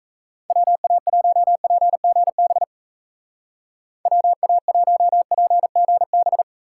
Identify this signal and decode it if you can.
Morse code, heard as a keyed tone, about 34 wpm, 700 Hz. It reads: WA1PGB WA1PGB
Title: Speed Racing - USA 2x3 Call Signs 28wpm